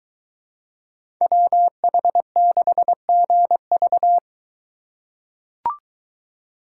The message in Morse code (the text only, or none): WH6GV